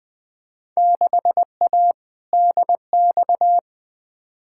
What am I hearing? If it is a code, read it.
Morse code, heard as a keyed tone, about 20 wpm, 700 Hz. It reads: 6A DX
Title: Field Day Exchanges Encore - 20wpm